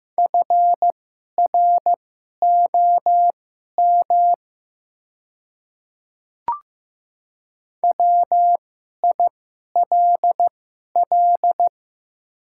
Morse code, heard as a keyed tone, about 15 wpm, spaced as for 11 wpm, 700 Hz. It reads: FROM WILL
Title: 2x Character Spacing - Top 100 Words - 15wpm